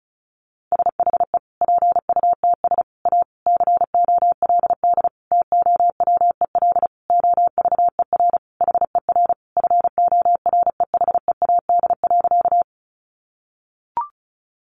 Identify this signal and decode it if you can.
Morse code, heard as a keyed tone, about 34 wpm, 700 Hz. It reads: SHE PUTS A COLD TOWEL OVER HER FOREHEAD.
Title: Easy Phrases 35wpm